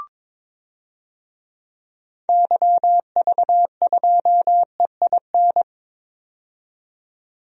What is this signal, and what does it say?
Morse code, heard as a keyed tone, about 22 wpm, 700 Hz. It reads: YV2EIN
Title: DXCC-100 Call Signs - 6 Characters - 22wpm